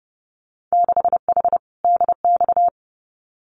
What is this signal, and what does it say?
Morse code, heard as a keyed tone, about 32 wpm, 700 Hz. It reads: 6H DX
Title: Winter Field Day Exchanges Encore 30wpm